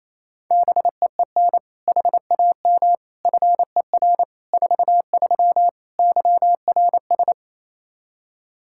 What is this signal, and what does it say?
Morse code, heard as a keyed tone, about 28 wpm, 700 Hz. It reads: BEEN HAM FER 43 YRS
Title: Practice QSO - Exchange 3 - More Info - 28wpm